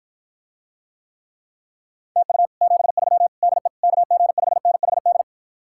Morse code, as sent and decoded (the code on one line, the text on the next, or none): - ..- / --... ...-- / -.. / -.- --. ..... -. .... -..
TU 73 DE KG5NHD